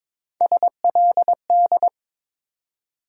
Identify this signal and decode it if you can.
Morse code, heard as a keyed tone, about 22 wpm, 700 Hz. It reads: SLD
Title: USA Call Sign Postfixes - 3 Letters 22wpm